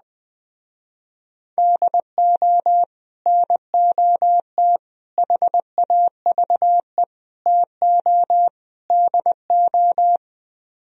Morse code, heard as a keyed tone, about 20 wpm, 700 Hz. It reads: DO NOT HAVE TO DO
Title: Speed-Racing - Sets of 5 Words 17wpm